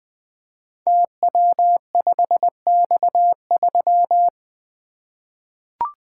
Morse code, heard as a keyed tone, about 20 wpm, 700 Hz. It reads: TW5X3
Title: Code Groups - 5 Letter:Numbers - 20wpm